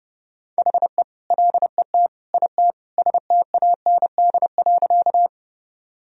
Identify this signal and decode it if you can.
Morse code, heard as a keyed tone, about 32 wpm, 700 Hz. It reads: HE LET IT STAND.
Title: Sentences from Top 1000 Words - Review Part 1 - 30wpm